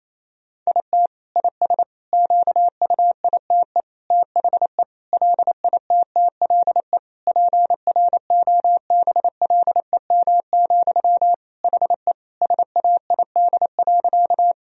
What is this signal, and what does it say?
Morse code, heard as a keyed tone, about 28 wpm, 700 Hz. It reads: IT IS QUITE THE LITTLE PROBLEM, HE SAID.
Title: Sentences from Top 1000 Words - Review Part 2 - 28wpm